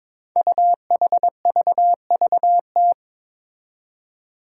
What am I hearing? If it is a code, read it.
Morse code, heard as a keyed tone, about 22 wpm, 700 Hz. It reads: UHVVT
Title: Five Challenging Letter-Number Combinations 22wpm